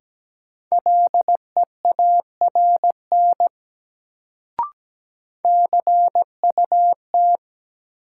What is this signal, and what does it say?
Morse code, heard as a keyed tone, about 17 wpm, 700 Hz. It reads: LEARN CUT
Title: Sentences from Top 300 Words - New Word Drill - 17wpm